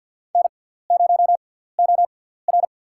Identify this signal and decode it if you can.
Morse code, heard as a keyed tone, about 53 wpm, spaced as for 17 wpm, 700 Hz. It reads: N0OR
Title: USA Call Signs - 6x Character Spacing - 50wpm